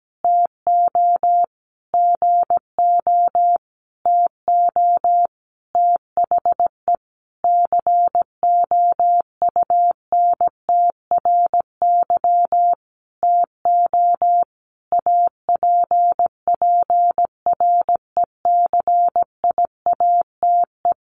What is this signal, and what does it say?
Morse code, heard as a keyed tone, about 17 wpm, 700 Hz. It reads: TO GO TO THE COUNTRY TO APPRECIATE
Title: Sentences from Top 1300 Words 17wpm